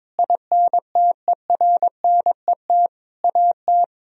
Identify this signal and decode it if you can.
Morse code, heard as a keyed tone, about 22 wpm, 700 Hz. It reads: INTERNET AT
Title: Sentences from Top 1000 Words - Review Part 4 - 22wpm